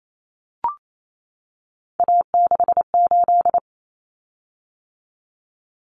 Morse code, heard as a keyed tone, about 28 wpm, 700 Hz. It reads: A68
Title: Common Cut Numbers - 3-Digit 28wpm